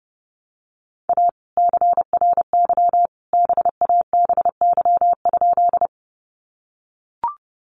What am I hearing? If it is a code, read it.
Morse code, heard as a keyed tone, about 30 wpm, 700 Hz. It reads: A CRY BABY?
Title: Sentences from Top 1300 Words 30wpm